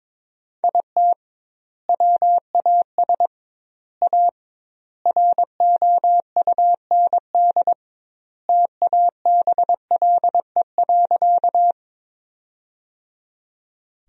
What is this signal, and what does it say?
Morse code, heard as a keyed tone, about 22 wpm, 700 Hz. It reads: IT WAS A ROUND TABLE.
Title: 2x Word-Spacing - Sentences from Top 400 Words - 22wpm